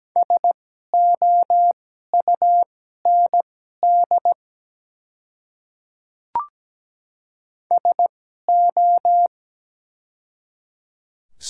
Morse code, heard as a keyed tone, about 17 wpm, spaced as for 12 wpm, 700 Hz. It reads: SOUND SO
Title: 2x Character Spacing - Top 100 Words - 17wpm